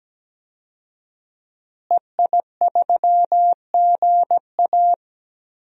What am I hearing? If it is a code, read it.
Morse code, heard as a keyed tone, about 17 wpm, 700 Hz. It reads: EI3GA
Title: DXCC-100 Call Signs - 5 Characters - Repeated 2x - 17wpm